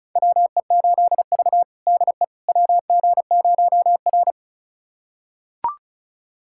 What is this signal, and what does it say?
Morse code, heard as a keyed tone, about 36 wpm, 700 Hz. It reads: WE8V DE WG0R
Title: USA Callsign DE Callsign - 4 Characters - 35wpm